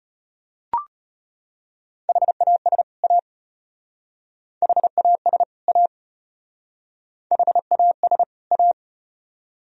Morse code, heard as a keyed tone, about 33 wpm, 700 Hz. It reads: HAS A HAS A HAS A